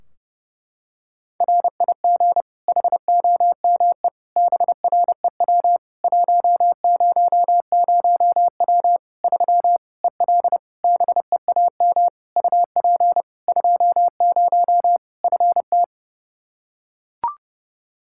Morse code, heard as a keyed tone, about 32 wpm, 700 Hz. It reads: RIG HOME BREW 100W 3 EL BEAM UP 20 FT
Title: Practice QSO - Exchange 2 - Info - 30wpm